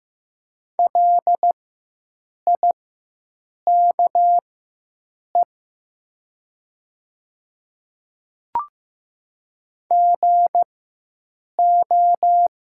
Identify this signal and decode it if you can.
Morse code, heard as a keyed tone, about 15 wpm, spaced as for 7 wpm, 700 Hz. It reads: LIKE GO
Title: 4x Character Spacing - Top 100 Words - 15wpm